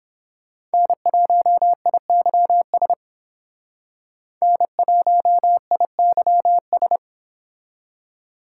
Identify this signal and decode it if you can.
Morse code, heard as a keyed tone, about 28 wpm, 700 Hz. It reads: N1IYS N1IYS